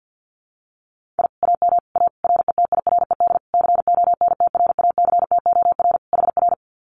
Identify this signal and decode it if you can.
Morse code, heard as a keyed tone, about 50 wpm, 700 Hz. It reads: I AM A RETIRED CONTRACTOR HR
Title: Common QSO Phrases Encore 50wpm